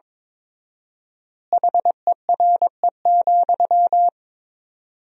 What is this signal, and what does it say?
Morse code, heard as a keyed tone, about 22 wpm, 700 Hz. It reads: HERE,
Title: 4x Word-Spacing - Sentences from Top 300 Words - 22wpm